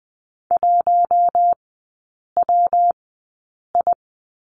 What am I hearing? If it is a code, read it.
Morse code, heard as a keyed tone, about 20 wpm, spaced as for 8 wpm, 700 Hz. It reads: 1WI
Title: 152 Lesson 25e - Letter p - Three Characters 20wpm